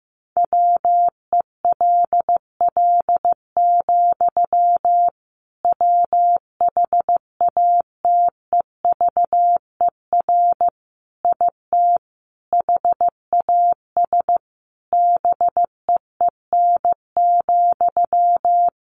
Morse code, heard as a keyed tone, about 15 wpm, 700 Hz. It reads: WELL, WHATEVER IT HAS BEEN,
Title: Sentences from Top 1100 Words 15wpm